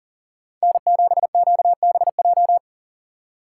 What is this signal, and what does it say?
Morse code, heard as a keyed tone, about 52 wpm, 700 Hz. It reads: N7QBJ